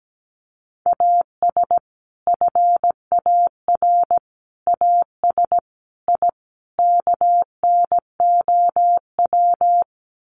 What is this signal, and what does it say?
Morse code, heard as a keyed tone, about 17 wpm, 700 Hz. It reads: AS FAR AS I KNOW
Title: Sets of 5 Words 17wpm